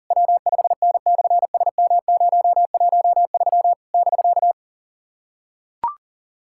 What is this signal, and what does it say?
Morse code, heard as a keyed tone, about 37 wpm, 700 Hz. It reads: W5N/SM013 <BK>
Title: SOTA QSO Practice 40wpm